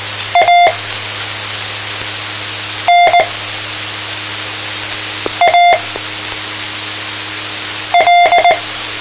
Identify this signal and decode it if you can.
Morse code, heard as a keyed tone, about 19 wpm, spaced as for 4 wpm, 700 Hz. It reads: ANAL